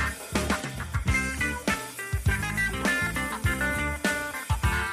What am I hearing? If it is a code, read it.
Emergency Alert System (SAME) tones, heard as harsh buzzing bursts. It is heard from 1.1 to 3.9 s.
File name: wrct-log hi- 06